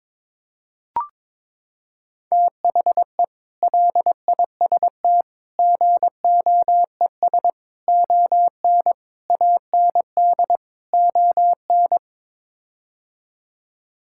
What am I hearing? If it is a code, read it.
Morse code, heard as a keyed tone, about 22 wpm, 700 Hz. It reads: THE LIST GOES ON AND ON
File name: practice / 22wpm / Sets of 6 Words - 22wpm